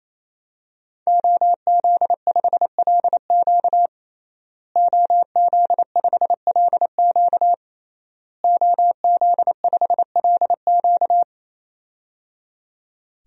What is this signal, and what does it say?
Morse code, heard as a keyed tone, about 28 wpm, 700 Hz. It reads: OZ5LQ OZ5LQ OZ5LQ
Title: DXCC-100 Call Signs - 5 Characters - Repeated 3x - 28wpm